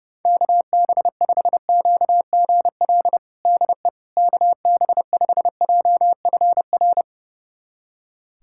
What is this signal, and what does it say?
Morse code, heard as a keyed tone, about 30 wpm, 700 Hz. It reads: KB5QGL DE KB5JFR
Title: USA Callsign DE Callsign - 6 Characters - 30wpm